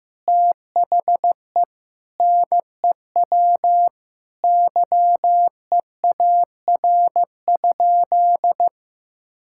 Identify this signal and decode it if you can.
Morse code, heard as a keyed tone, about 15 wpm, 700 Hz. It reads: THE NEW YEAR?
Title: Sentences from Top 1000 Words - Review Part 5 - 15wpm